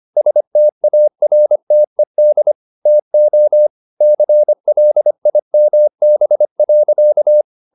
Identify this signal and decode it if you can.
Morse code, heard as a keyed tone, about 25 wpm, 600 Hz. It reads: STARTED TO CLIMB.